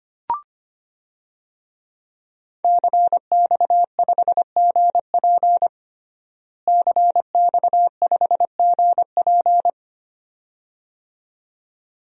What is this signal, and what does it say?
Morse code, heard as a keyed tone, about 25 wpm, 700 Hz. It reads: CX5GP CX5GP
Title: DXCC-100 Call Signs - 5 Characters - Repeated 2x - 25wpm